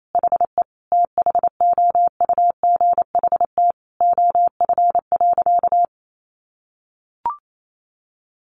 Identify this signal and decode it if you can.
Morse code, heard as a keyed tone, about 28 wpm, 700 Hz. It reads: HE THOUGHT OF.
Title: Sentences from Top 300 Words 28wpm